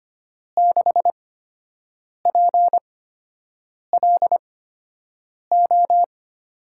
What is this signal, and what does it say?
Morse code, heard as a keyed tone, about 25 wpm, spaced as for 7 wpm, 700 Hz. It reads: 6PLO
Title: USA Call Signs - 6 Characters - 8x Character Spacing - 25wpm